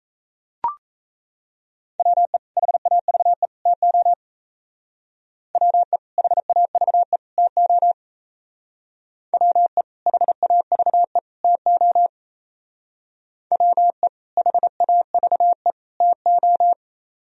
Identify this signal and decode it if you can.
Morse code, heard as a keyed tone, about 36 wpm, 700 Hz. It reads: WE HAVE TO WE HAVE TO WE HAVE TO WE HAVE TO